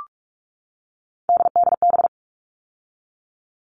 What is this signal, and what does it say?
Morse code, heard as a keyed tone, about 44 wpm, 700 Hz. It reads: DDB
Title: All Cut Numbers - 3-Digit 45wpm